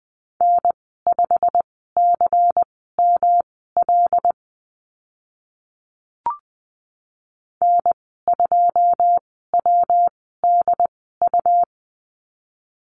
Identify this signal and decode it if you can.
Morse code, heard as a keyed tone, about 20 wpm, spaced as for 14 wpm, 700 Hz. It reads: N5CML N2WDU
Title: 179 Lesson 29g - Slash - Call Signs 20wpm